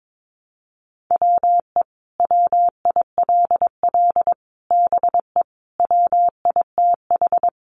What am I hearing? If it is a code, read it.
Morse code, heard as a keyed tone, about 22 wpm, 700 Hz. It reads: WE WILL BE WITH
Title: Sentences from Top 600 Words 22wpm